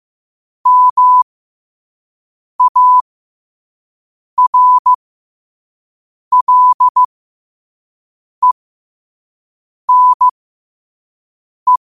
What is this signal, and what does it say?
Morse code, heard as a keyed tone, about 15 wpm, spaced as for 5 wpm, 1000 Hz. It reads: MARLENE
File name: qotdfiles / 5wpm